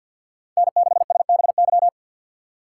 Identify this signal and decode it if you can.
Morse code, heard as a keyed tone, about 52 wpm, 700 Hz. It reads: N6SBY
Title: USA Call Signs - 5 Characters - 6x Character Spacing - 50wpm